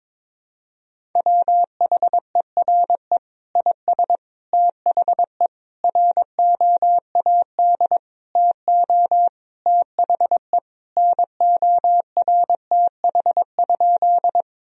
Morse code, heard as a keyed tone, about 22 wpm, 700 Hz. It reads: WHERE IS THE ROAD TO THE NORTH?
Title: Sentences from Top 1000 Words - Review Part 2 - 22wpm